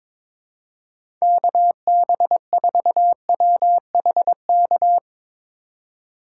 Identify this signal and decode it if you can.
Morse code, heard as a keyed tone, about 22 wpm, 700 Hz. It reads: KB4WHK
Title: USA Call Signs - 6 Characters - 2x Character Spacing - 22wpm